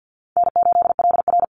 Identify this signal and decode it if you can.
Morse code, heard as a keyed tone, about 49 wpm, 700 Hz. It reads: N8LR
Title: Easy Call Signs Repeated 3x - 50wpm